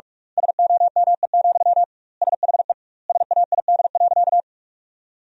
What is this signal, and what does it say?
Morse code, heard as a keyed tone, about 46 wpm, 700 Hz. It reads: SOME, SHE SAID.
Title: 2x Word-Spacing - Sentences from Top 200 Words - 45wpm